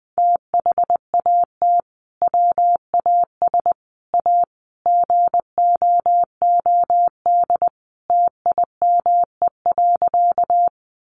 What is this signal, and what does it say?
Morse code, heard as a keyed tone, about 20 wpm, 700 Hz. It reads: THAT WAS A GOOD TIME.
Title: Sentences from Top 200 Words 20wpm